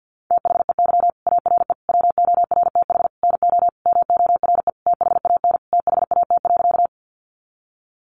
Morse code, heard as a keyed tone, about 50 wpm, 700 Hz. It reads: THEY ARE WORTH NO MORE THAN THAT.
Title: Sentences from Top 800 Words 50wpm